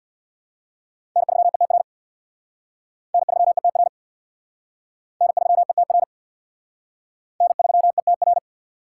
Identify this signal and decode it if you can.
Morse code, heard as a keyed tone, about 64 wpm, 700 Hz. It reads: N3ETR N3ETR N3ETR N3ETR